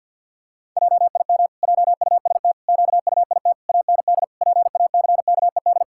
Morse code, heard as a keyed tone, about 50 wpm, 700 Hz. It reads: JIM JUST QUIT AND PACKED
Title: Pangram Challenge v2 - 50wpm